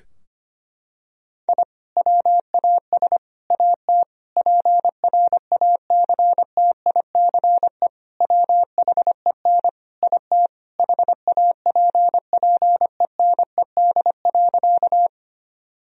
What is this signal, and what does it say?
Morse code, heard as a keyed tone, about 25 wpm, 700 Hz. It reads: I WAS AT PRACTICE WHEN IT HAPPENED.
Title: Sentences from Top 1000 Words - Review Part 4 - 25wpm